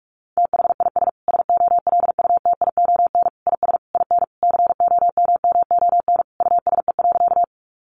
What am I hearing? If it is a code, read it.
Morse code, heard as a keyed tone, about 44 wpm, 700 Hz. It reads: THIS SOLUTION IS IN COMMON USE.
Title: Sentences from Top 1300 Words 45wpm